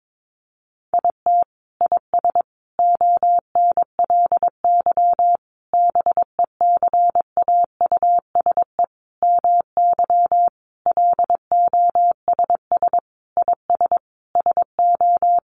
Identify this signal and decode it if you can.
Morse code, heard as a keyed tone, about 22 wpm, 700 Hz. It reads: IT IS ONLY BECAUSE MY LOSS IS SO
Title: Sentences from Top 1400 Words 22wpm